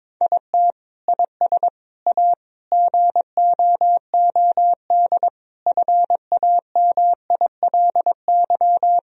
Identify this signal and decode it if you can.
Morse code, heard as a keyed tone, about 22 wpm, 700 Hz. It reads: IT IS A GOOD FAMILY
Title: Sentences from Top 1000 Words - Review Part 2 - 22wpm